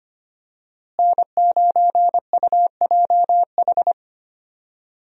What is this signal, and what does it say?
Morse code, heard as a keyed tone, about 25 wpm, 700 Hz. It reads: N9UJH
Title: USA Call Signs - 5 Characters - Repeated 3x - 25wpm